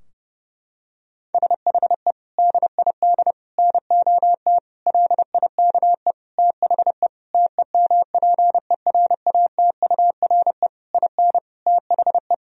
Morse code, heard as a keyed tone, about 32 wpm, 700 Hz. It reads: SHE DID NOT LIKE THE TEMPERATURE IN THE
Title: Sentences from Top 1000 Words - Review Part 4 - 30wpm